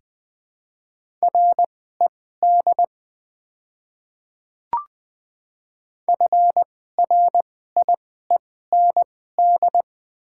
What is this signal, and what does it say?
Morse code, heard as a keyed tone, about 20 wpm, spaced as for 14 wpm, 700 Hz. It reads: RED FRIEND
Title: 2x Character Spacing - Top 500 Words - 20wpm